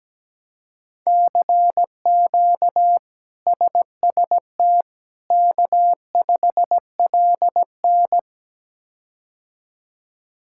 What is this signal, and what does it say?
Morse code, heard as a keyed tone, about 17 wpm, 700 Hz. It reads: CQ SST K5LN
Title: K1USN SST QSO Practice - 17wpm